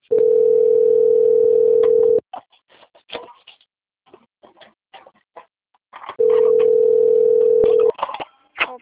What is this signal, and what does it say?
Ringback tone (North America), 440 + 480 Hz.